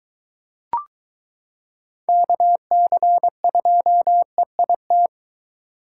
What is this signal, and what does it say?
Morse code, heard as a keyed tone, about 23 wpm, 700 Hz. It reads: KC2EIT